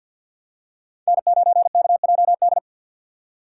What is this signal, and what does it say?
Morse code, heard as a keyed tone, about 47 wpm, 700 Hz. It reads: N9KJD
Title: USA Call Signs - 6x Character Spacing - 50wpm